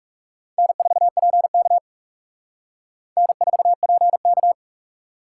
Morse code, heard as a keyed tone, about 40 wpm, 700 Hz. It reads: N4PK N4PK